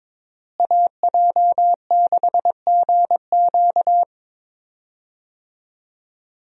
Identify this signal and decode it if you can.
Morse code, heard as a keyed tone, about 22 wpm, 700 Hz. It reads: AJ6GQ
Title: USA Call Signs - 5 Characters - Repeated 3x - 22wpm